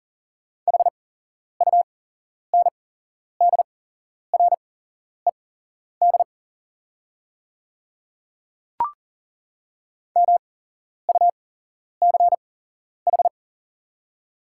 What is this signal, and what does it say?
Morse code, heard as a keyed tone, about 40 wpm, spaced as for 11 wpm, 690 Hz. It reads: HUNDRED MUCH
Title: 8x Character Spacing - Top 500 Words - 40wpm